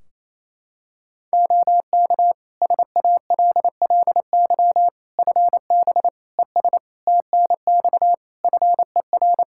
Morse code, heard as a keyed tone, about 28 wpm, 700 Hz. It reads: OK SALLY FB ES TNX FER
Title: Practice QSO - Exchange 2 - Info - 28wpm